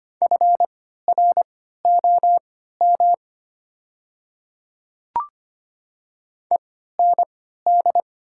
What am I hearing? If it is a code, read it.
Morse code, heard as a keyed tone, about 25 wpm, spaced as for 14 wpm, 690 Hz. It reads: FROM END
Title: 3x Character Spacing - Top 200 Words - 25wpm